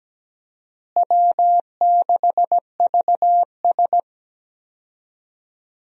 Morse code, heard as a keyed tone, about 17 wpm, 700 Hz. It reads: W6VS
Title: USA Call Signs - 4 Characters - Repeated 3x - 17wpm